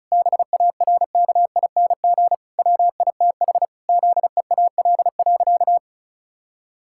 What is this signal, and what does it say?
Morse code, heard as a keyed tone, about 34 wpm, 700 Hz. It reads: BARKING WITH ZEAL.
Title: Pangram Challenge v2 - 35wpm